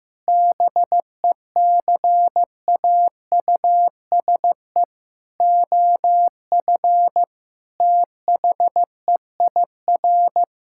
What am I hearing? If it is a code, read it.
Morse code, heard as a keyed tone, about 15 wpm, 700 Hz. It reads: BECAUSE OF THEIR
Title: Sentences from Top 1000 Words - Review Part 5 - 15wpm